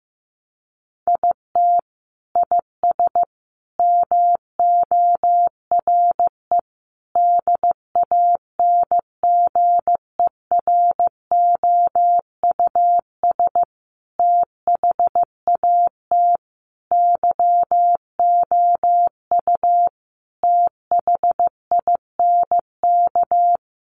Morse code, heard as a keyed tone, about 15 wpm, 700 Hz. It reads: IT IS MORE DANGEROUS THAT YOU THINK
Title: Sentences from Top 1200 Words 15wpm